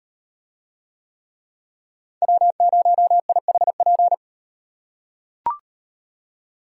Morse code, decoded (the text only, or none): W0IHP